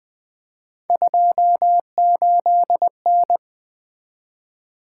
Morse code, heard as a keyed tone, about 24 wpm, 700 Hz. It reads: 28N
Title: Speed-Racing - Three Letter-Number Combinations - 20wpm